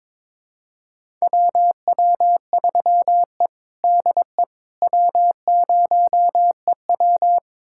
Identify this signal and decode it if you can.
Morse code, heard as a keyed tone, about 22 wpm, 700 Hz. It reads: WW3E DE W0EW
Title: USA Callsign DE Callsign - 4 Characters - 22wpm